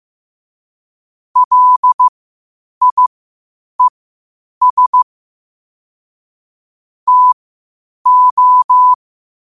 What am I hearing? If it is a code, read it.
Morse code, heard as a keyed tone, about 15 wpm, spaced as for 8 wpm, 1000 Hz. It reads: LIES TO
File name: qotdfiles / 8wpm